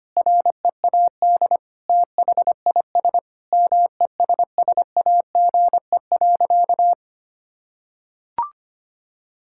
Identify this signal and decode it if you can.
Morse code, heard as a keyed tone, about 25 wpm, 700 Hz. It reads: READ THIS MESSAGE.
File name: practice / 25wpm / Sentences with 9-Word Phrases - 25wpm